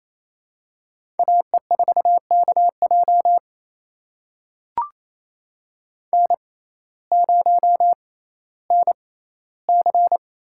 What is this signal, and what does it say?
Morse code, heard as a keyed tone, about 27 wpm, spaced as for 15 wpm, 700 Hz. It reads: AE4KJ N0NC
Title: USA Call Signs - 5 Characters - 6x Character Spacing - 28wpm